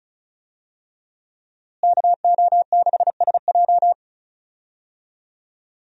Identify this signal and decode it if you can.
Morse code, heard as a keyed tone, about 34 wpm, 700 Hz. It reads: KO6SJ
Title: USA Call Signs - 5 Characters - Repeated 2x - 35wpm